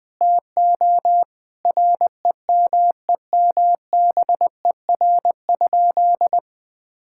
Morse code, heard as a keyed tone, about 21 wpm, 700 Hz. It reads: TO REMEMBER?
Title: Sentences from Top 1000 Words - Review Part 6 - 20wpm